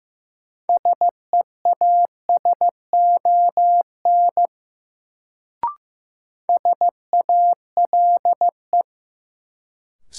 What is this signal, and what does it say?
Morse code, heard as a keyed tone, about 15 wpm, 700 Hz. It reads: SEASON SALE
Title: Sentences from Top 1400 Words - New Word Drill Alt - 15wpm